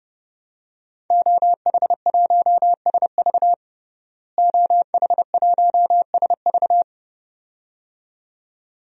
Morse code, heard as a keyed tone, about 32 wpm, 700 Hz. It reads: OH1SV OH1SV
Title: DXCC-100 Call Signs - 5 Characters - Repeated 2x - 30wpm